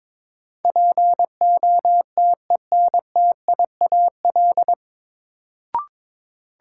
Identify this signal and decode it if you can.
Morse code, heard as a keyed tone, about 22 wpm, 700 Hz. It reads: POTENTIAL
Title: Sentences from Top 1000 Words - Review Part 4 - New Word Drill - 22wpm